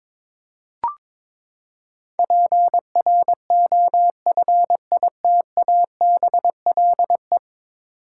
Morse code, heard as a keyed tone, about 21 wpm, 700 Hz. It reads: PROFITABLE
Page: morse-code-ninja-practice-22wpm